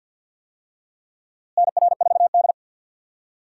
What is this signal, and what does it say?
Morse code, heard as a keyed tone, about 51 wpm, 700 Hz. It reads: NR4D
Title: USA Call Signs - 4 Characters - Repeated 2x - 50wpm